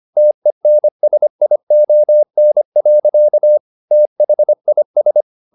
Morse code, heard as a keyed tone, about 25 wpm, 600 Hz. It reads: TENSION. THIS
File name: the old man and the sea part 1 25wpm